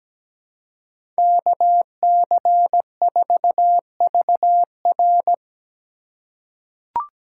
Morse code, heard as a keyed tone, about 17 wpm, 700 Hz. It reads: KC4VR
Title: USA Call Signs - 5 Characters - 3x Character Spacing - 17wpm